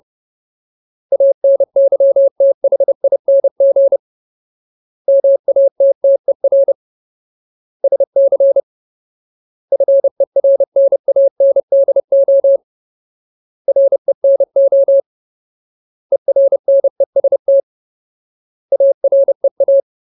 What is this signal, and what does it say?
Morse code, heard as a keyed tone, about 30 wpm, 550 Hz. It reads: ANYTHING MATTER SC FERNANDO RENO ERNEST AREA
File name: Ultimate Word List / 30WPM Ultimate